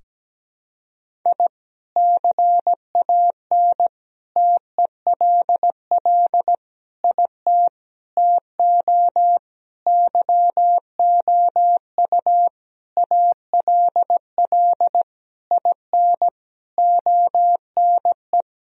Morse code, heard as a keyed tone, about 17 wpm, 700 Hz. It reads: I CAN TELL IT TO YOU ALL IN ONE
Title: Sentences from Top 1000 Words - Review Part 1 - 17wpm